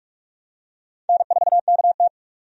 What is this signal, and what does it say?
Morse code, heard as a keyed tone, about 46 wpm, 700 Hz. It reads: N4KT
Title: USA Call Signs - 4 Characters - 45wpm